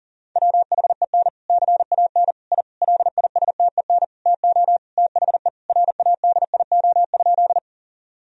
Morse code, heard as a keyed tone, about 40 wpm, 700 Hz. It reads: WHEN CAN I LISTEN TO THE RADIO?
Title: Sentences from Top 1000 Words - Review Part 3 - 40wpm